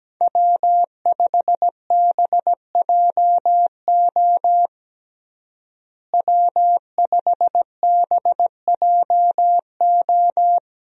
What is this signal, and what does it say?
Morse code, heard as a keyed tone, about 17 wpm, 700 Hz. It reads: W5BJO W5BJO
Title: USA Call Signs - 5 Characters - Repeated 3x - 17wpm